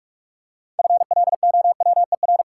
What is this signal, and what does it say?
Morse code, heard as a keyed tone, about 46 wpm, 700 Hz. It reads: FLOWER
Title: Six Letter Words - 45wpm